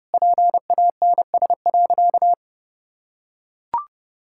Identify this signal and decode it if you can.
Morse code, heard as a keyed tone, about 30 wpm, 700 Hz. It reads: PANS.
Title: Pangram Challenge v2 - 30wpm